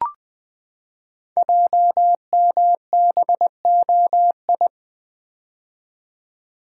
Morse code, heard as a keyed tone, about 20 wpm, 700 Hz. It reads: JMBOI
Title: Code Groups - 5 Letters - 20wpm